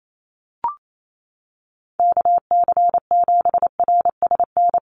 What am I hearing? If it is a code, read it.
Morse code, heard as a keyed tone, about 28 wpm, 700 Hz. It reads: KC7RSN
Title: USA Call Signs Repeated 2x 28wpm